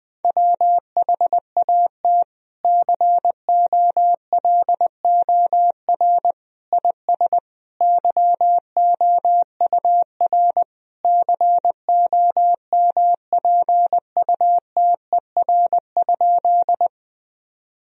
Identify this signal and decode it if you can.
Morse code, heard as a keyed tone, about 20 wpm, 700 Hz. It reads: WHAT COLOR IS YOUR COMPUTER?
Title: Sentences from Top 1000 Words - Review Part 3 - 20wpm